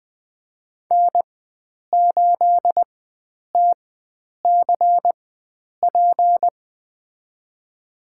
Morse code, heard as a keyed tone, about 20 wpm, spaced as for 9 wpm, 700 Hz. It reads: N8TCP
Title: USA Call Signs - 5 Characters - 4x Character Spacing - 20wpm